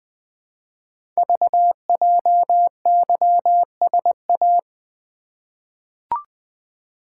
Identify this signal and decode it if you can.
Morse code, heard as a keyed tone, about 20 wpm, 700 Hz. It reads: VJYSA
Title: Code Groups - 5 Letters - 20wpm